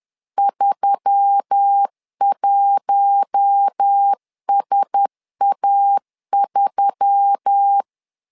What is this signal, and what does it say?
Morse code, heard as a keyed tone, about 10 wpm, 800 Hz. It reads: 31SA3